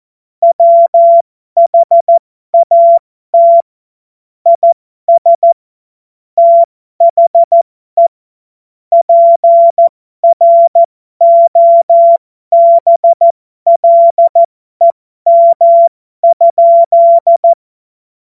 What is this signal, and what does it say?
Morse code, heard as a keyed tone, about 14 wpm, spaced as for 12 wpm, 680 Hz. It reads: WHAT IS THE PROBLEM?
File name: Morsekurs / Witze